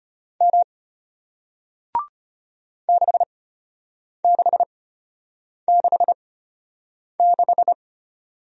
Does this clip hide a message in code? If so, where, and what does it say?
Yes: Morse code from 0.4 to 7.7 s, heard as a keyed tone, about 35 wpm, spaced as for 8 wpm, 700 Hz. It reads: M 6666